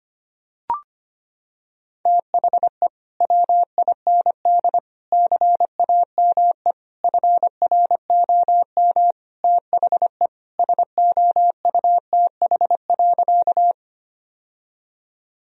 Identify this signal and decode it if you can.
Morse code, heard as a keyed tone, about 25 wpm, 700 Hz. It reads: THE WIND CAME FROM THE SOUTH.
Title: Sentences from Top 1000 Words - Review Part 2 - 25wpm